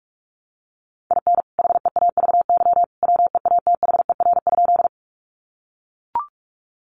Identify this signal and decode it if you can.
Morse code, heard as a keyed tone, about 44 wpm, 700 Hz. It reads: IN HEAVY WEATHER?
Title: Sentences from Top 1100 Words 45wpm